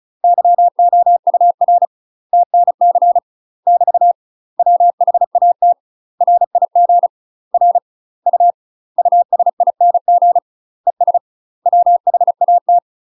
Morse code, heard as a keyed tone, about 36 wpm, 700 Hz. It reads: YOUR TNC <BT> WHAT RIG R U USING ES WHAT
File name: UK MASTER QSOs TRAINING / 35WPM / UK MASTER QSOs 0010